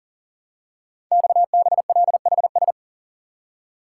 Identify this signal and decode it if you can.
Morse code, heard as a keyed tone, about 39 wpm, 700 Hz. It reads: XBLHS
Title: Code Groups - 5 Letters - 40wpm